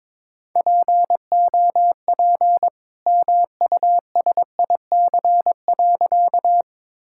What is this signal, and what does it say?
Morse code, heard as a keyed tone, about 22 wpm, 700 Hz. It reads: POP MUSIC.
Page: morse-code-ninja-practice-22wpm